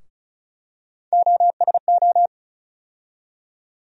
Morse code, heard as a keyed tone, about 34 wpm, 700 Hz. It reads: OSO
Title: USA Call Sign Postfixes - 3 Letters 35wpm